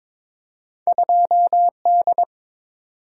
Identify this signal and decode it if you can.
Morse code, heard as a keyed tone, about 22 wpm, 700 Hz. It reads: 2D
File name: practice / 22wpm / Two Letter-Number Combinations - 22wpm